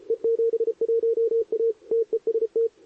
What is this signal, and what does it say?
Morse code, heard as a keyed tone, about 31 wpm, 440 Hz, with radio noise. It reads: E71A TEST
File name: CQWWCW2023 / SP